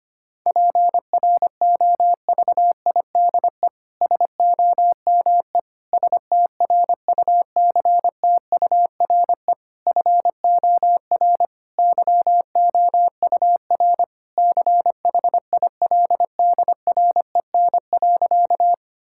Morse code, heard as a keyed tone, about 25 wpm, 700 Hz. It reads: PROVIDE SOME STRUCTURE FOR YOUR CHILDREN.
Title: Sentences from Top 1000 Words - Review Part 6 - 25wpm